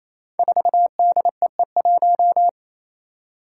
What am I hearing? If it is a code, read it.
Morse code, heard as a keyed tone, about 29 wpm, 700 Hz. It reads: 4DEE1
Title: Five Challenging Letter-Number Combinations 28wpm